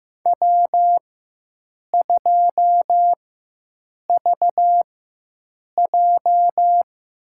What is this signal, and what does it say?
Morse code, heard as a keyed tone, about 15 wpm, spaced as for 7 wpm, 700 Hz. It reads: W2VJ
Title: USA Call Signs - 4 Characters - 4x Character Spacing - 15wpm